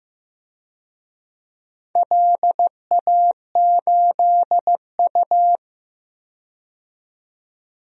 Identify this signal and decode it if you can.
Morse code, heard as a keyed tone, about 15 wpm, 700 Hz. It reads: LA8U
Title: DXCC-100 Call Signs - 4 Characters - 15wpm